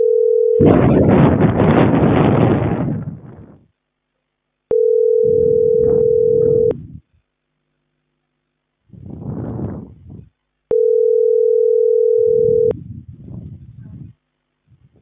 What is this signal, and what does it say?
Ringback tone (North America), 440 + 480 Hz.